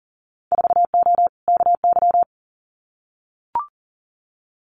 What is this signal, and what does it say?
Morse code, heard as a keyed tone, about 37 wpm, 700 Hz. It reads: 4O KY
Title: Winter Field Day Exchanges Encore 40wpm